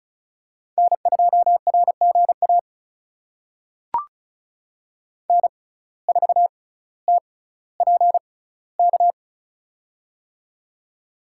Morse code, heard as a keyed tone, about 36 wpm, spaced as for 18 wpm, 700 Hz. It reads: N2RGA N4TPK
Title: USA Call Signs - 5 Characters - 6x Character Spacing - 35wpm